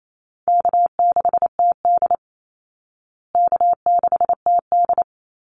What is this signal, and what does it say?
Morse code, heard as a keyed tone, about 28 wpm, 700 Hz. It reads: K6TD K6TD
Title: Easy Call Signs Repeated 2x - 28wpm